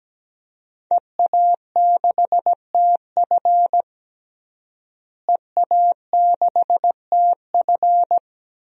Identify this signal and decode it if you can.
Morse code, heard as a keyed tone, about 17 wpm, 700 Hz. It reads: EA6TF EA6TF
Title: DXCC-100 Call Signs - 5 Characters - Repeated 3x - 17wpm